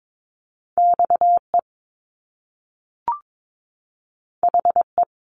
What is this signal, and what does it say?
Morse code, heard as a keyed tone, about 22 wpm, 700 Hz. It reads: XE HE
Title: Top Two Letter Combinations in Words 22wpm